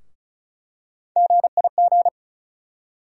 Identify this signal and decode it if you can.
Morse code, heard as a keyed tone, about 36 wpm, 700 Hz. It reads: GIG
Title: USA Call Sign Postfixes - 3 Letters 35wpm